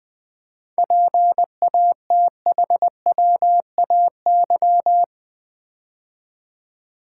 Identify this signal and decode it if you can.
Morse code, heard as a keyed tone, about 20 wpm, 700 Hz. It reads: PATHWAY
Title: 7 Letter Words - First 3 Letters Given - 20wpm